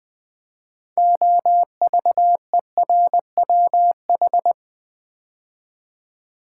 Morse code, heard as a keyed tone, about 20 wpm, 700 Hz. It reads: OVERWH
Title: Build A Word - 3 Letters At A Time - 9-Letter Words - 20wpm